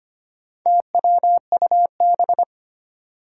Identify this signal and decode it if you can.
Morse code, heard as a keyed tone, about 25 wpm, 700 Hz. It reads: TWUB
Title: Four Challenging Letter-Number Combinations 25wpm